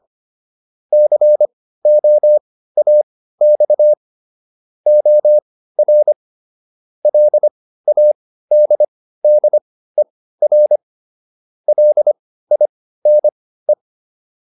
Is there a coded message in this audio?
Yes: Morse code from 0.9 to 13.7 s, heard as a keyed tone, about 25 wpm, spaced as for 15 wpm, 600 Hz. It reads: COAX OR LADDER LINE